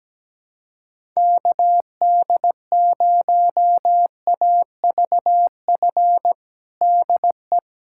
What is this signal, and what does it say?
Morse code, heard as a keyed tone, about 17 wpm, 700 Hz. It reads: KD0AVF DE
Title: USA Callsign DE Callsign 17wpm